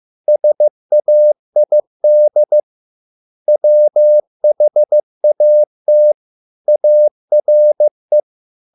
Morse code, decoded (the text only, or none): SAID WHAT ARE